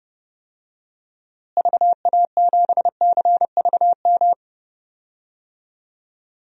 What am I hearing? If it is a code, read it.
Morse code, heard as a keyed tone, about 32 wpm, 700 Hz. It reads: VA7CVM
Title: DXCC-100 Call Signs - 6 Characters - Repeated 2x - 30wpm